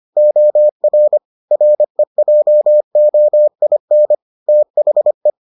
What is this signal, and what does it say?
Morse code, heard as a keyed tone, about 25 wpm, 600 Hz. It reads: OR REJOIN THE